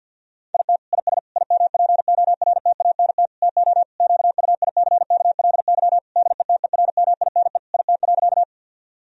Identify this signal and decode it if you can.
Morse code, heard as a keyed tone, about 50 wpm, 700 Hz. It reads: IT IS IMPORTANT TO QUICKLY DETERMINE IT.
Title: Sentences from Top 1000 Words - Review Part 4 - 50wpm